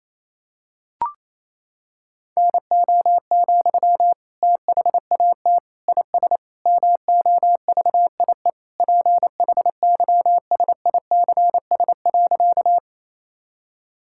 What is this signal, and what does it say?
Morse code, heard as a keyed tone, about 27 wpm, 700 Hz. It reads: NO, THAT IS MOVIE PHYSICS.
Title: Sentences from Top 1000 Words - Review Part 5 - 28wpm